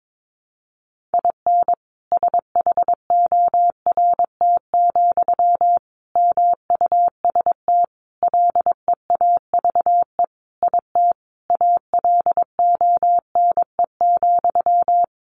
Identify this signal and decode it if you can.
Morse code, heard as a keyed tone, about 22 wpm, 700 Hz. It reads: IN SHORT, MUST LEAVE IT ALONE,
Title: Sentences from Top 1000 Words 22wpm